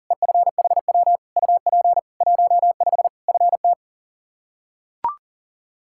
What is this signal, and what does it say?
Morse code, heard as a keyed tone, about 37 wpm, 700 Hz. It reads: EFHW UP 15 FT
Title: Practice QSO - Exchange 2 - Info - 40wpm